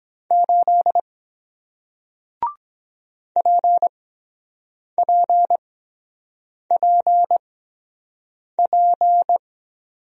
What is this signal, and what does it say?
Morse code, heard as a keyed tone, about 23 wpm, spaced as for 7 wpm, 700 Hz. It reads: 8 PPPP